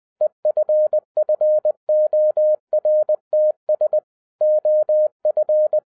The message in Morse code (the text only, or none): EFFORTS OF